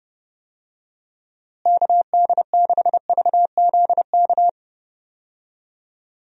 Morse code, heard as a keyed tone, about 32 wpm, 700 Hz. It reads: KD6VZK
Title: USA Call Signs - 6 Characters - Repeated 3x - 30wpm